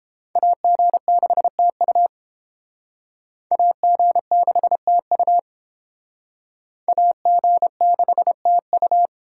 Morse code, heard as a keyed tone, about 28 wpm, 700 Hz. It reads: AG6TU AG6TU AG6TU